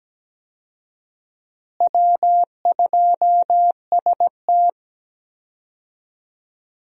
Morse code, heard as a keyed tone, about 17 wpm, 700 Hz. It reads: W2ST
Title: USA Call Signs - 4 Characters - Repeated 2x - 17wpm